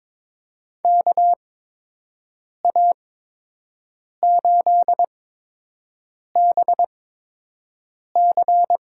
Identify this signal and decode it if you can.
Morse code, heard as a keyed tone, about 22 wpm, spaced as for 6 wpm, 700 Hz. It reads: KA8BC
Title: USA Call Signs - 6 Characters - 8x Character Spacing - 22wpm